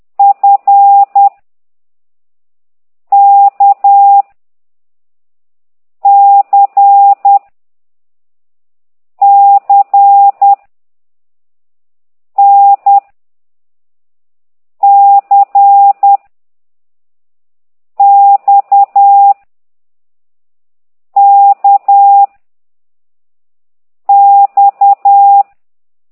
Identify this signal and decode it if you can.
Morse code, heard as a keyed tone, about 10 wpm, 800 Hz. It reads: F K C C N C X K X
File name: ct2idw / exerc9